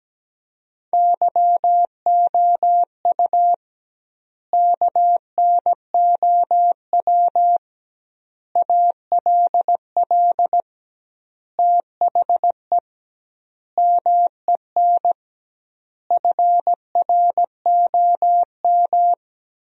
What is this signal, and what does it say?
Morse code, heard as a keyed tone, about 17 wpm, 700 Hz. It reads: YOU KNOW ALL THE MEN FROM
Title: 2x Word-Spacing - Sentences from Top 300 Words - 17wpm